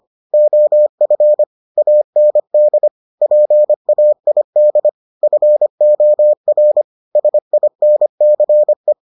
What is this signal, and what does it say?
Morse code, heard as a keyed tone, about 25 wpm, 600 Hz. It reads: OF AND PAID FOR SINCE